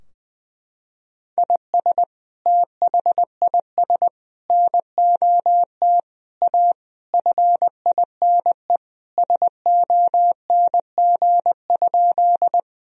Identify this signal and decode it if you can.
Morse code, heard as a keyed tone, about 20 wpm, 700 Hz. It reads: IS THIS NOT A FINE SONG?
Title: Sentences from Top 1000 Words - Review Part 2 - 20wpm